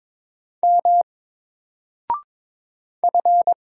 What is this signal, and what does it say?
Morse code, heard as a keyed tone, about 22 wpm, 700 Hz. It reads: M F
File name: Single-Letter-Number-22wpm